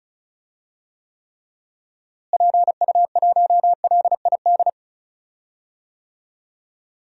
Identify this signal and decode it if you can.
Morse code, heard as a keyed tone, about 36 wpm, 700 Hz. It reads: PU1LID
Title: DXCC-100 Call Signs - 6 Characters - Repeated 3x - 35wpm